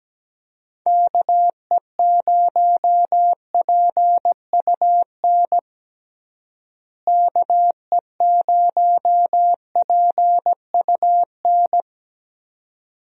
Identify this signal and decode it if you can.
Morse code, heard as a keyed tone, about 17 wpm, 700 Hz. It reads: KE0PUN KE0PUN
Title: USA Call Signs - 6 Characters - Repeated 2x - 17wpm